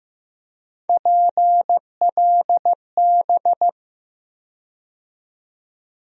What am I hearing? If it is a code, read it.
Morse code, heard as a keyed tone, about 15 wpm, 700 Hz. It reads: PLB
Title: USA Call Sign Postfixes - 3 Letters 15wpm